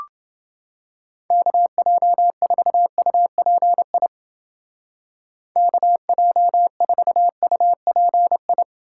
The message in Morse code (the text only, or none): KJ4UPI KJ4UPI